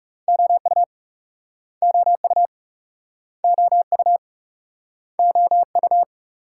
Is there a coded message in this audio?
Yes: Morse code from 0.3 to 6.0 s, heard as a keyed tone, about 40 wpm, 700 Hz. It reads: OU OU OU OU